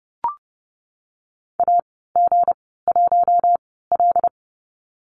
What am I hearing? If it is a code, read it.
Morse code, heard as a keyed tone, about 32 wpm, spaced as for 17 wpm, 700 Hz. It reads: AG1L
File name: Lesson 37g - Letter x - Call Signs 30wpm